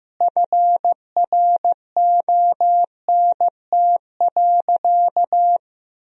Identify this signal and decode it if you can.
Morse code, heard as a keyed tone, about 15 wpm, 700 Hz. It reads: FRONT.
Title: Sentences from Top 1000 Words - Review Part 2 - 15wpm